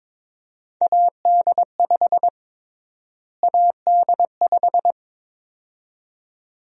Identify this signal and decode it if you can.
Morse code, heard as a keyed tone, about 22 wpm, 700 Hz. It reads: AD5 AD5
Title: USA Call Sign Prefixes - Repeated 2x - 22wpm